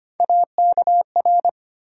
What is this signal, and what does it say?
Morse code, heard as a keyed tone, about 25 wpm, 700 Hz. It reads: AKR
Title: USA Call Sign Postfixes - 3 Letters 25wpm